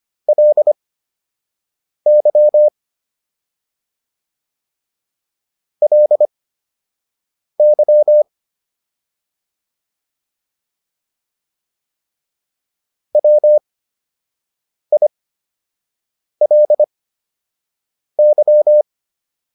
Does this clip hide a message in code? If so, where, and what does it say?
Yes: Morse code from 0.3 to 18.8 s, heard as a keyed tone, about 25 wpm, spaced as for 6 wpm, 600 Hz. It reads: LY LY WILY